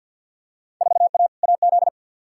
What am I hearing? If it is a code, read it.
Morse code, heard as a keyed tone, about 53 wpm, 700 Hz. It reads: 4A AZ
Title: Field Day Exchanges - 50wpm